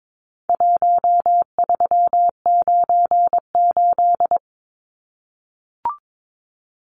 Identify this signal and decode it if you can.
Morse code, heard as a keyed tone, about 22 wpm, 700 Hz. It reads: 1398
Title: Numbers - Contest Style - 4-Digit 22wpm